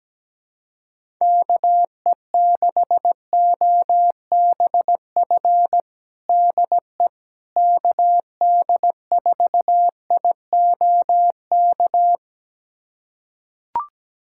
Morse code, heard as a keyed tone, about 17 wpm, 700 Hz. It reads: KE6OBF DE KD4IOK
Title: USA Callsign DE Callsign 17wpm